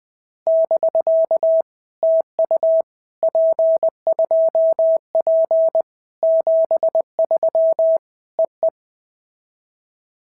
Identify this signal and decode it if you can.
Morse code, heard as a keyed tone, about 20 wpm, 650 Hz. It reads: <BK> TU P2P 73 EE
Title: POTA QSO Practice - 20wpm